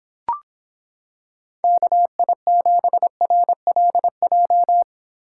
Morse code, heard as a keyed tone, about 25 wpm, 700 Hz. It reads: KI7RLJ